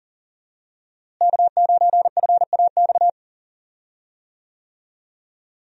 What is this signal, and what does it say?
Morse code, heard as a keyed tone, about 40 wpm, 700 Hz. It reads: K9FAX
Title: USA Call Signs - 5 Characters - Repeated 3x - 40wpm